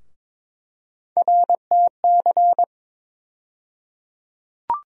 Morse code, heard as a keyed tone, about 22 wpm, 700 Hz. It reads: RTC
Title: USA Call Sign Postfixes - 3 Letters 22wpm